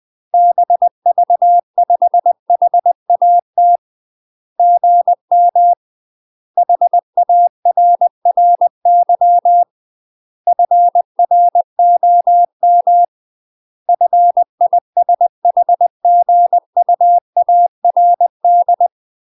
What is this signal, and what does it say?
Morse code, heard as a keyed tone, about 20 wpm, 700 Hz. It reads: BV5HAT GM HARRY FROM FISHGUARD
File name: UK MASTER QSOs TRAINING / 20WPM / UK MASTER QSOs 0049